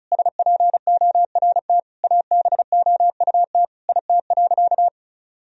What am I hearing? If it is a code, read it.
Morse code, heard as a keyed tone, about 36 wpm, 700 Hz. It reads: SPORT ABOUT IT.
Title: Sentences from Top 1000 Words - Review Part 4 - 35wpm